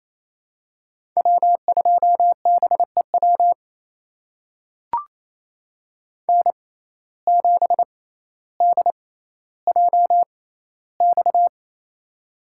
Morse code, heard as a keyed tone, about 28 wpm, spaced as for 14 wpm, 700 Hz. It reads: W2BEW N7DJX
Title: USA Call Signs - 5 Characters - 6x Character Spacing - 28wpm